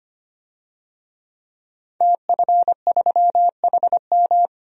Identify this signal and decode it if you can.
Morse code, heard as a keyed tone, about 25 wpm, 700 Hz. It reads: TF3HM
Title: DXCC-100 Call Signs - 5 Characters - 25wpm